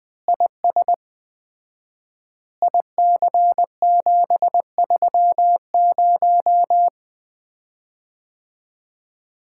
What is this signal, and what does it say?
Morse code, heard as a keyed tone, about 20 wpm, 700 Hz. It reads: IS IC730
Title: Common QSO Phrases - 4x Word-Spacing - 20wpm